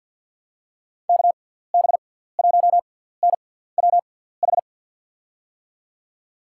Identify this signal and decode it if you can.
Morse code, heard as a keyed tone, about 53 wpm, spaced as for 17 wpm, 700 Hz. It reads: KB1NWH
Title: USA Call Signs - 6 Characters - 6x Character Spacing - 50wpm